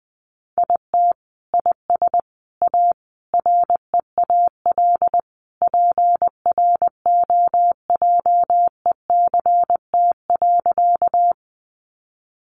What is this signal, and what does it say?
Morse code, heard as a keyed tone, about 20 wpm, 700 Hz. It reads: IT IS A REAL PROJECT.
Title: Sentences from Top 1400 Words 20wpm